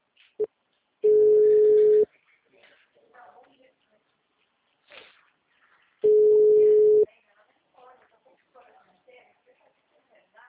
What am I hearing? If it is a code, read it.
Ringback tone (Europe, 425 Hz), with speech.